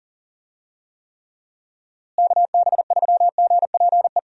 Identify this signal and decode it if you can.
Morse code, heard as a keyed tone, about 40 wpm, 700 Hz. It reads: KB3GPE
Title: USA Call Signs - 6 Characters - Repeated 2x - 40wpm